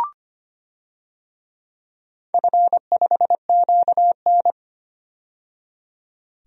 Morse code, heard as a keyed tone, about 25 wpm, 700 Hz. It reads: F5QN
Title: DXCC-100 Call Signs - 4 Characters - 25wpm